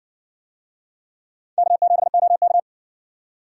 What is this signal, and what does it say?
Morse code, heard as a keyed tone, about 60 wpm, 700 Hz. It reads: K7OK